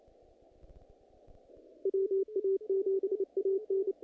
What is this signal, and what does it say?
Morse code, heard as a keyed tone, about 27 wpm, 390 Hz. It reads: WA7AN
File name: contests / naqp-cw-k5tr-op-k5pi- -2000